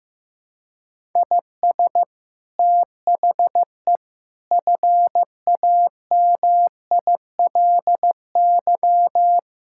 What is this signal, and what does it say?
Morse code, heard as a keyed tone, about 16 wpm, 700 Hz. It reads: IS THE FAMILY
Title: Sentences from Top 1000 Words - Review Part 5 - 15wpm